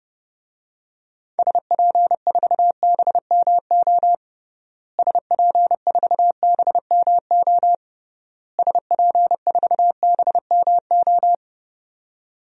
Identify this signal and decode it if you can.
Morse code, heard as a keyed tone, about 32 wpm, 700 Hz. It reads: SP4BMO SP4BMO SP4BMO
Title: DXCC-100 Call Signs - 6 Characters - Repeated 3x - 30wpm